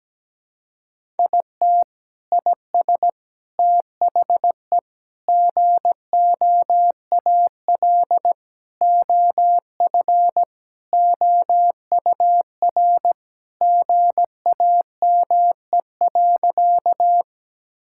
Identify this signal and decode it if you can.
Morse code, heard as a keyed tone, about 17 wpm, 700 Hz. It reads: IT IS THE GOAL OF OUR GAME.
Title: Sentences from Top 1000 Words - Review Part 5 - 17wpm